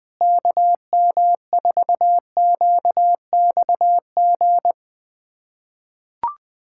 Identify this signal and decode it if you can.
Morse code, heard as a keyed tone, about 20 wpm, 700 Hz. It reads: KM4QXG
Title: USA Callsign DE Callsign - 6 Characters - 20wpm